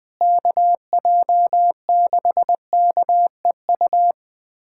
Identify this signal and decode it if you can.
Morse code, heard as a keyed tone, about 20 wpm, 700 Hz. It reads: KJ6KEU
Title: USA Callsign DE Callsign - 6 Characters - 20wpm